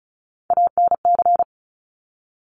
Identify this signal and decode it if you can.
Morse code, heard as a keyed tone, about 36 wpm, 700 Hz. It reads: ANC
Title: Top Three Letter Combinations in Words 35wpm